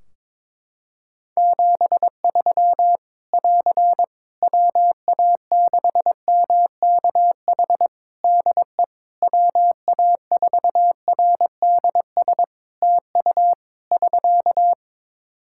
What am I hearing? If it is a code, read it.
Morse code, heard as a keyed tone, about 22 wpm, 700 Hz. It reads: 73 <AR> WA6MKH DE WA4RDS TU <SK>
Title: Practice QSO - Exchange 4 - Ending - 22wpm